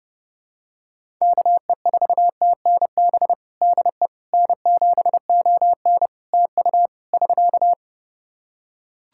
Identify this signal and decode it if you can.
Morse code, heard as a keyed tone, about 32 wpm, 700 Hz. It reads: KE4TNB DE N7ON TU <SK>
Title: Practice QSO - Exchange 4 - Ending - 30wpm